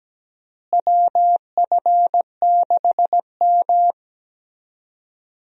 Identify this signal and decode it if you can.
Morse code, heard as a keyed tone, about 17 wpm, 700 Hz. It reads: WF6M
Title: USA Call Signs - 4 Characters - Repeated 3x - 17wpm